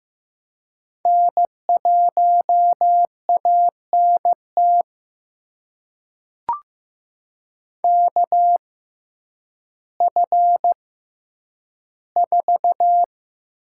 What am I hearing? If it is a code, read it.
Morse code, heard as a keyed tone, about 16 wpm, spaced as for 9 wpm, 700 Hz. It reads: N1ANT KF4
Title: USA Call Signs - 5 Characters - 6x Character Spacing - 15wpm